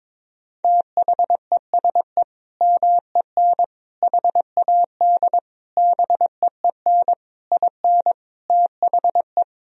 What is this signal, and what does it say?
Morse code, heard as a keyed tone, about 22 wpm, 700 Hz. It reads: THESE MEN HAD BEEN IN THE
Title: Sentences from Top 1000 Words - Review Part 1 - 22wpm